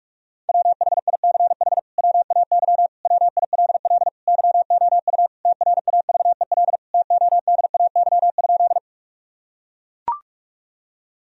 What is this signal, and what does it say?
Morse code, heard as a keyed tone, about 41 wpm, 700 Hz. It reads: WHICH WAY WILL YOU TRAVEL TODAY?
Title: Sentences from Top 1000 Words - Review Part 2 - 45wpm